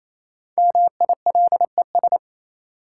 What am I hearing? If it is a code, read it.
Morse code, heard as a keyed tone, about 28 wpm, 700 Hz. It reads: MILES
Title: Sentences from Top 1000 Words - Review Part 3 - New Word Drill - 28wpm